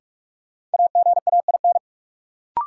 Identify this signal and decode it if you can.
Morse code, heard as a keyed tone, about 44 wpm, 700 Hz. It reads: AGAIN
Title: Sets of 4 Words - New Word Drill - 45wpm